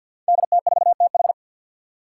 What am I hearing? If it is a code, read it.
Morse code, heard as a keyed tone, about 49 wpm, 700 Hz. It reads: DT4TH
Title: Five Challenging Letter-Number Combinations 50wpm